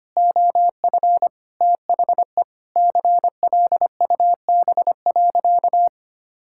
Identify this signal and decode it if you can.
Morse code, heard as a keyed tone, about 25 wpm, 700 Hz. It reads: OF THE CLUB.
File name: practice / 25wpm / Sentences with 3-Word Phrases - 25wpm